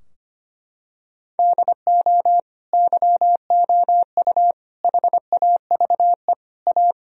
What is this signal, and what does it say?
Morse code, heard as a keyed tone, about 25 wpm, 700 Hz. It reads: DO YOU HAVE A
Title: Sentences from Top 1000 Words - Review Part 4 - 25wpm